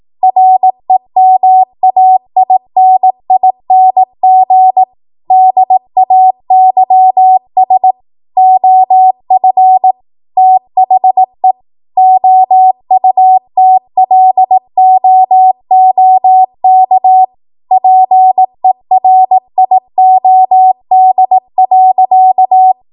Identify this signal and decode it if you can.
Morse code, heard as a keyed tone, about 18 wpm, 750 Hz. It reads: REMAINING DAYS OF THE OUTLOOK PERIOD.
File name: Morse / 18WPM